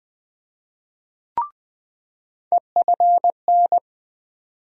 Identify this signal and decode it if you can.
Morse code, heard as a keyed tone, about 20 wpm, 700 Hz. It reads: EFN
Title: USA Call Sign Postfixes - 3 Letters 20wpm